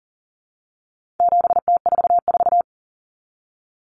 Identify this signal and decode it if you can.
Morse code, heard as a keyed tone, about 40 wpm, 700 Hz. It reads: 7T44
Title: Common Cut Numbers - 4-Digit 40wpm